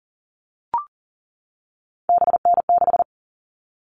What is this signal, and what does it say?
Morse code, heard as a keyed tone, about 40 wpm, 700 Hz. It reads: BN6
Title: All Cut Numbers - 3-Digit 40wpm